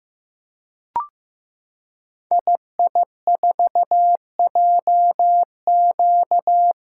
Morse code, heard as a keyed tone, about 15 wpm, 700 Hz. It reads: II4JQ
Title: Code Groups - 5 Letter:Numbers - 15wpm